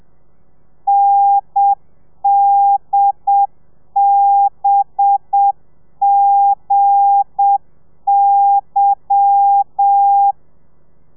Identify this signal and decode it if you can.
Morse code, heard as a keyed tone, about 7 wpm, 800 Hz. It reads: NDBGY
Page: Lezione 1: audio a 30 CPM